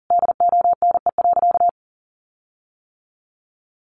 Morse code, heard as a keyed tone, about 40 wpm, 700 Hz. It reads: DONE.
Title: Sentences from Top 1300 Words 40wpm